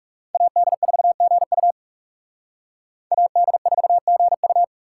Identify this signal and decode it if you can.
Morse code, heard as a keyed tone, about 36 wpm, 700 Hz. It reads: AD4GU AD4GU
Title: Speed Racing - USA 2x2 Call Signs 30wpm